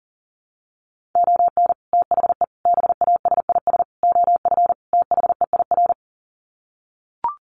Morse code, heard as a keyed tone, about 37 wpm, 700 Hz. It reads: ON THE BASIS OF THEIR
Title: Sets of 5 Words Encore 40wpm